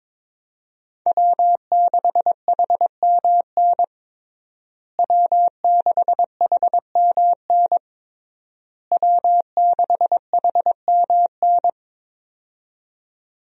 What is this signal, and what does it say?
Morse code, heard as a keyed tone, about 22 wpm, 700 Hz. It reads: W6HMN W6HMN W6HMN
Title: USA Call Signs - 5 Characters - Repeated 3x - 22wpm